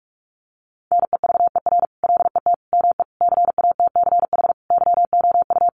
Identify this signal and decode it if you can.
Morse code, heard as a keyed tone, about 44 wpm, 700 Hz. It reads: NEVER LET ME CATCH YOU
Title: Sentences from Top 1500 Words 45wpm